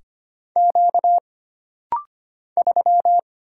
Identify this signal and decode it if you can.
Morse code, heard as a keyed tone, about 25 wpm, 700 Hz. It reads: Q 3
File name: Single Character (Letter:Number:Punctuation:Prosign) - Rapid-Fire - 25wpm